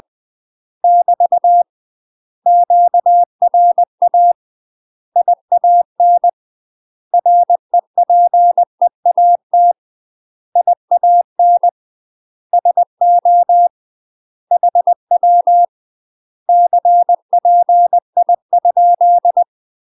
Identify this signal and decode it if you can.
Morse code, heard as a keyed tone, about 20 wpm, 700 Hz. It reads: <BT> QRA IAN REPEAT IAN SO HW CPI?
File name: UK MASTER QSOs TRAINING / 20WPM / UK MASTER QSOs 0031